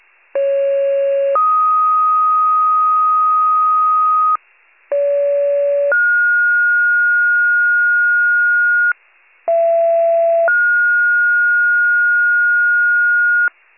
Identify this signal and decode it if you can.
Two-tone page, heard as a one-second tone, then a three-second tone, about 550 then 1250 Hz, about 570 then 1430 Hz, about 670 then 1430 Hz, with noise. It is heard from 0.3 to 13.5 s.
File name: pager / 39 FIRE / wcalert